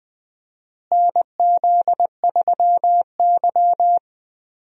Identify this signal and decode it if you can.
Morse code, heard as a keyed tone, about 20 wpm, 700 Hz. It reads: NZ3Y
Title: USA Call Signs - 4 Characters - Repeated 2x - 20wpm